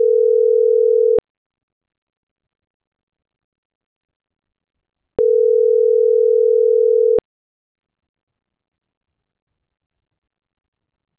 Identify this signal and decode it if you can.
Ringback tone (North America), 440 + 480 Hz.
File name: CENTRAL GENERAL / out saby.reyes